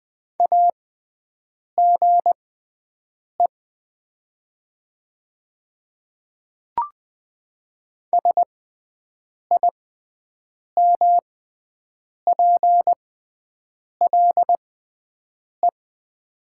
Morse code, heard as a keyed tone, about 20 wpm, spaced as for 7 wpm, 700 Hz. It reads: AGE SIMPLE
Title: 6x Character Spacing - Top 500 Words - 20wpm